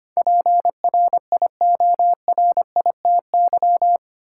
Morse code, heard as a keyed tone, about 25 wpm, 700 Hz. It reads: PRIORITY
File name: Eight Letter Words - 25wpm